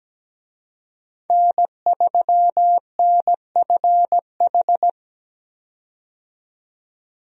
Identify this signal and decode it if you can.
Morse code, heard as a keyed tone, about 17 wpm, 700 Hz. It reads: N3NFH
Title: USA Call Signs - 5 Characters - 17wpm